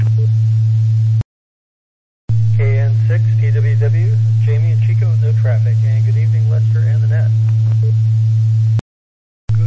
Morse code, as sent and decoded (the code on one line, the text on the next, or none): none